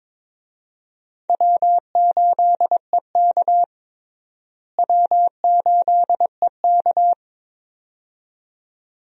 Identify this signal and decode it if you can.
Morse code, heard as a keyed tone, about 22 wpm, 700 Hz. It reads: W8EK W8EK
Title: USA Call Signs - 4 Characters - Repeated 2x - 22wpm